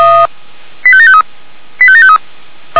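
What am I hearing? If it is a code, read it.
Five-tone selective call (ZVEI), heard as quick runs of five tones, tones of about 70 ms. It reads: 85753 85753
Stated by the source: ZVEI five-tone selective call received on an SDR.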